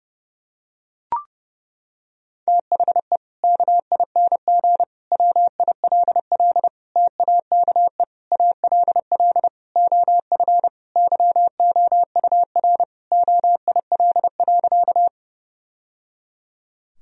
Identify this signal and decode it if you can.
Morse code, heard as a keyed tone, about 30 wpm, 700 Hz. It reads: THE KING WILL TAKE ALL OF YOUR OIL.
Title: Sentences from Top 1000 Words - Review Part 4 - 30wpm